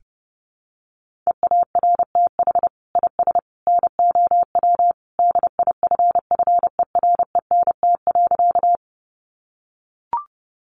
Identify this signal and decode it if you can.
Morse code, heard as a keyed tone, about 30 wpm, 700 Hz. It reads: EARTH IS NOW DIFFERENT.
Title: Sentences from Top 600 Words 30wpm